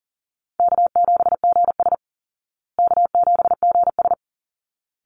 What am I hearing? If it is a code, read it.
Morse code, heard as a keyed tone, about 40 wpm, 700 Hz. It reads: K7GS K7GS
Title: Call Signs Repeated 3x - 40wpm